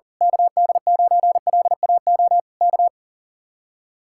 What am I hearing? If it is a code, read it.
Morse code, heard as a keyed tone, about 35 wpm, 700 Hz. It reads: KD9LAO K
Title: Speed-Racing Common QSO Phrases Encore - 40wpm